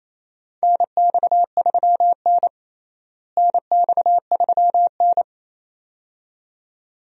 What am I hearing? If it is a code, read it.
Morse code, heard as a keyed tone, about 27 wpm, 700 Hz. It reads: NX3N NX3N
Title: USA Call Signs - 4 Characters - Repeated 2x - 28wpm